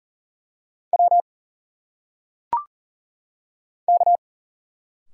Morse code, heard as a keyed tone, about 40 wpm, 700 Hz. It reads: W K
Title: Single Letters - Phonetic Pronunciation 40wpm